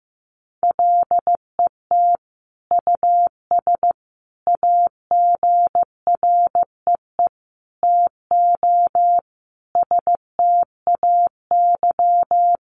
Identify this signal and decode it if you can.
Morse code, heard as a keyed tone, about 14 wpm, 700 Hz. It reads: LET US AGREE TO STAY
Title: Sentences from Top 1100 Words 15wpm